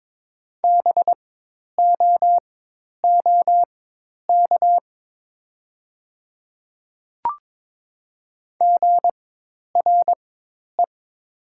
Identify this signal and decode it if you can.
Morse code, heard as a keyed tone, about 22 wpm, spaced as for 10 wpm, 700 Hz. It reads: BOOK GRE
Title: 4x Character Spacing - Top 500 Words - 22wpm